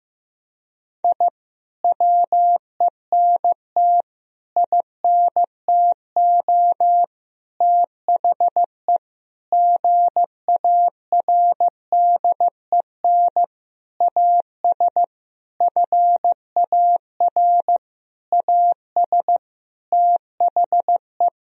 Morse code, heard as a keyed tone, about 15 wpm, 700 Hz. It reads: I WENT INTO THE GARDEN AS FAR AS THE
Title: Sentences from Top 1000 Words - Review Part 4 - 15wpm